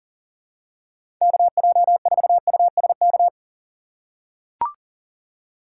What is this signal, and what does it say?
Morse code, heard as a keyed tone, about 38 wpm, 700 Hz. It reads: KJ4USK
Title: USA Call Signs - 6 Characters - 2x Character Spacing - 40wpm